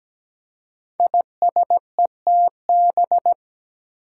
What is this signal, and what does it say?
Morse code, heard as a keyed tone, about 17 wpm, 700 Hz. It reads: ISETB
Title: Five Challenging Letter-Number Combinations 17wpm